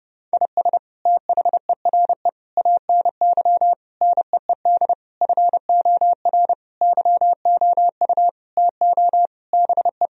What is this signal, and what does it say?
Morse code, heard as a keyed tone, about 30 wpm, 700 Hz. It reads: IS THERE ANY NEED FOR YOU TO BE
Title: Sentences from Top 1000 Words - Review Part 4 - 30wpm